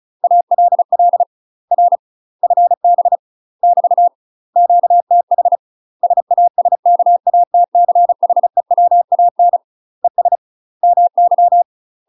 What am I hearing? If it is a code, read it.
Morse code, heard as a keyed tone, about 34 wpm, 700 Hz. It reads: ALL R FB <BT> QTH SASKATCHEWAN ES MY
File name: UK MASTER QSOs TRAINING / 35WPM / UK MASTER QSOs 0030